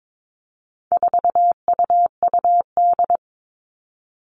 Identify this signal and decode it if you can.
Morse code, heard as a keyed tone, about 22 wpm, 700 Hz. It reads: 4UUD
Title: All Cut Numbers - 4-Digit 22wpm